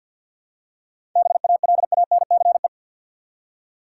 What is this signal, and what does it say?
Morse code, heard as a keyed tone, about 50 wpm, 700 Hz. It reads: BALANCE
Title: Seven Letter Words - 50wpm